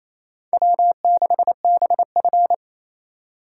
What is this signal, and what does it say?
Morse code, heard as a keyed tone, about 34 wpm, 700 Hz. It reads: W6BF
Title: Speed Racing - USA 1x2 Call Signs 28wpm